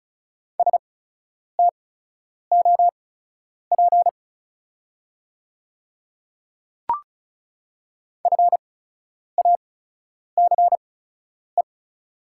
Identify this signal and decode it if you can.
Morse code, heard as a keyed tone, about 36 wpm, spaced as for 10 wpm, 700 Hz. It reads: STOP FACE
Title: 8x Character Spacing - Top 500 Words - 35wpm